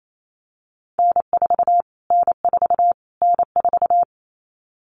Morse code, heard as a keyed tone, about 28 wpm, 700 Hz. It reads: N4 N4 N4
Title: Callsign Prefixes for Top 25 DXCC - Repeated 3x - 28wpm